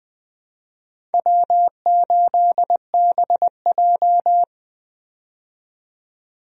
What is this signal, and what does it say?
Morse code, heard as a keyed tone, about 21 wpm, 700 Hz. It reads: W8BJ
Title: USA Call Signs - 4 Characters - 20wpm